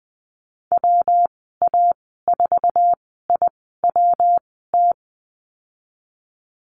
Morse code, heard as a keyed tone, about 20 wpm, spaced as for 15 wpm, 700 Hz. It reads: WA4IWT
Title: 122 Lesson 20g - Number 3 - Call Signs 20wpm